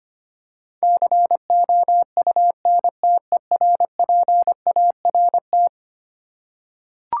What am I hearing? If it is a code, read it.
Morse code, heard as a keyed tone, about 25 wpm, 700 Hz. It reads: COUNTERPART
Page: morse-code-ninja-practice-25wpm